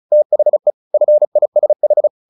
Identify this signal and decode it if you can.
Morse code, heard as a keyed tone, about 34 wpm, 600 Hz. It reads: THE FISH
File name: the old man and the sea / the old man and the sea 1 35wpm